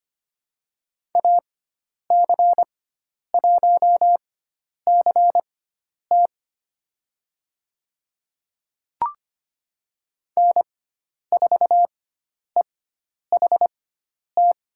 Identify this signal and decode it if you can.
Morse code, heard as a keyed tone, about 25 wpm, spaced as for 10 wpm, 700 Hz. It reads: AC1CT N4EHT
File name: Lesson 16g - Letter c - Call Signs 25wpm